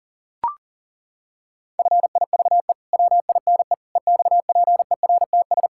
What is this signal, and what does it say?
Morse code, heard as a keyed tone, about 40 wpm, 700 Hz. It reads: FIVE WINE EXPERTS
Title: Pangram Challenge v2 - 40wpm